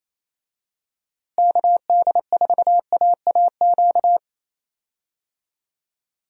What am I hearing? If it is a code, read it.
Morse code, heard as a keyed tone, about 28 wpm, 700 Hz. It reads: KD4AAQ
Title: USA Call Signs - 6 Characters - Repeated 3x - 28wpm